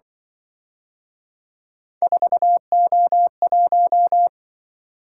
Morse code, heard as a keyed tone, about 23 wpm, 700 Hz. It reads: 4O1
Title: Speed-Racing - Three Letter-Number Combinations - 20wpm